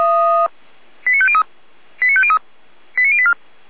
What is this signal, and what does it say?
Five-tone selective call (ZVEI), heard as quick runs of five tones, tones of about 70 ms. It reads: 89503 89503 89084